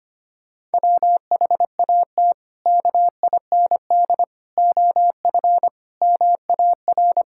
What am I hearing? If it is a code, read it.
Morse code, heard as a keyed tone, about 25 wpm, 700 Hz. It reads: WHAT KIND OF MAR
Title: Sentences from Top 1000 Words - Review Part 5 - 25wpm